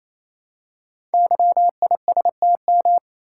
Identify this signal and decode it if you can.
Morse code, heard as a keyed tone, about 27 wpm, 700 Hz. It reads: YISTM
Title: Code Groups - 5 Letters - 28wpm